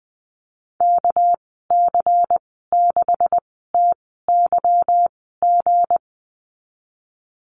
Morse code, heard as a keyed tone, about 20 wpm, spaced as for 15 wpm, 700 Hz. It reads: KC6TYG
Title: 200 Lesson 32g - Letter k - Call Signs 20wpm